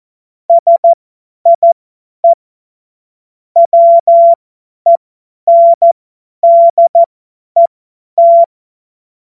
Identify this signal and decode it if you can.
Morse code, heard as a keyed tone, about 14 wpm, spaced as for 10 wpm, 680 Hz. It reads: SIE WENDET
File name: Morsekurs / Witze / Witz 23 Januar 2024 50 F70